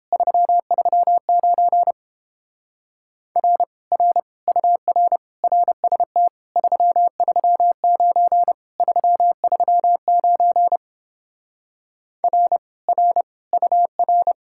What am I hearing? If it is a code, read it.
Morse code, heard as a keyed tone, about 29 wpm, 700 Hz. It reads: 339 R R UR RST 339 339 R R UR